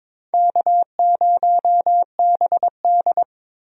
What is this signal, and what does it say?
Morse code, heard as a keyed tone, about 22 wpm, 700 Hz. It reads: K0BD
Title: USA Call Signs - 4 Characters - Repeated 2x - 22wpm